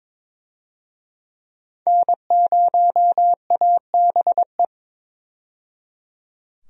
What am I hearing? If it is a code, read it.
Morse code, heard as a keyed tone, about 22 wpm, 700 Hz. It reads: N0ABE
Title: USA Call Signs - 5 Characters - 22wpm